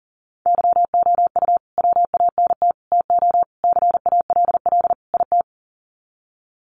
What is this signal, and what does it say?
Morse code, heard as a keyed tone, about 40 wpm, 700 Hz. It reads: YOU WANT TO CALL IT
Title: Sets of 5 Words Encore2 40wpm